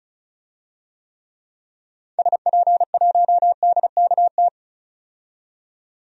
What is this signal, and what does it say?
Morse code, heard as a keyed tone, about 36 wpm, 700 Hz. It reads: SP1DKT
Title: DXCC-100 Call Signs - 6 Characters - 35wpm